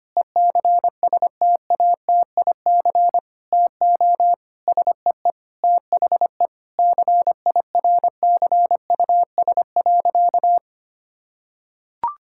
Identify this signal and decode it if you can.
Morse code, heard as a keyed tone, about 25 wpm, 700 Hz. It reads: ECSTATIC TO SEE THE CIRCUS.
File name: practice / 25wpm / Difficult Sentences with Hints - 25wpm